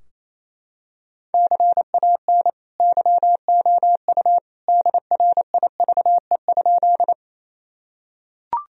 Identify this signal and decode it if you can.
Morse code, heard as a keyed tone, about 28 wpm, 700 Hz. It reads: CAN YOU DRIVE?
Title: Sentences from Top 1000 Words - Review Part 2 - 28wpm